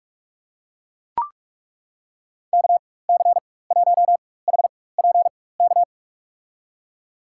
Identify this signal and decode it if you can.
Morse code, heard as a keyed tone, about 41 wpm, spaced as for 20 wpm, 700 Hz. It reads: KC1HPK
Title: USA Call Signs - 6 Characters - 4x Character Spacing - 45wpm